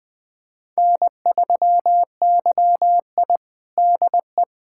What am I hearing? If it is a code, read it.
Morse code, heard as a keyed tone, about 20 wpm, 700 Hz. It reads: N3YI DE
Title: USA Callsign DE Callsign - 4 Characters - 20wpm